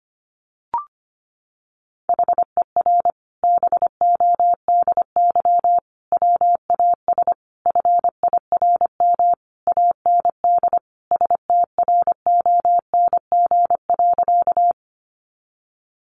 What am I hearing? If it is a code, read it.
Morse code, heard as a keyed tone, about 25 wpm, 700 Hz. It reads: HER BODY WAS FIRM AND STRONG.
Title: Sentences from Top 1400 Words 25wpm